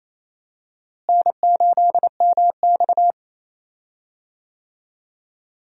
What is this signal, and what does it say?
Morse code, heard as a keyed tone, about 28 wpm, 700 Hz. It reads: N8MX
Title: USA Call Signs - 4 Characters - Repeated 2x - 28wpm